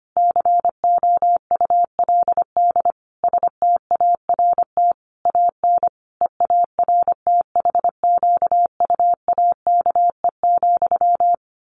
Morse code, heard as a keyed tone, about 25 wpm, 700 Hz. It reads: COULD START AN EARTHQUAKE,
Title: A Sound Of Thunder 25wpm